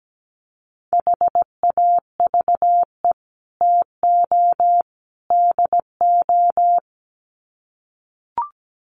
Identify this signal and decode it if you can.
Morse code, heard as a keyed tone, about 17 wpm, 700 Hz. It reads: HAVE TO DO
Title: Sets of 3 Words 17wpm